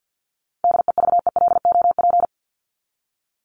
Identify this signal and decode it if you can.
Morse code, heard as a keyed tone, about 54 wpm, 700 Hz. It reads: DEVELOP
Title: ING Suffix - Top 100 Words with Base Words 50wpm